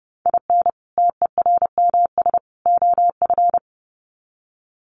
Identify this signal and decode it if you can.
Morse code, heard as a keyed tone, about 30 wpm, 700 Hz. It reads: IN TERMS OF
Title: Sets of 3 Words 30wpm